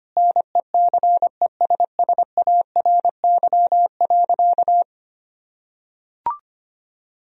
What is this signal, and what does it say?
Morse code, heard as a keyed tone, about 25 wpm, 700 Hz. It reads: NECESSARY.
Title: Sentences from Top 1000 Words - Review Part 4 - 25wpm